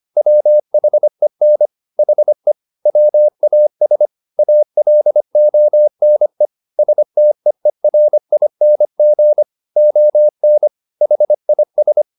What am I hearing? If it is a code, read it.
Morse code, heard as a keyed tone, about 25 wpm, 600 Hz. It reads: WHEN HE WAS ALONE STEERING ON HIS
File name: the old man and the sea part 1 25wpm